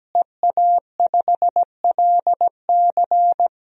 Morse code, heard as a keyed tone, about 17 wpm, 700 Hz. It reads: EA5LC
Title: DXCC-100 Call Signs - 5 Characters - Repeated 2x - 17wpm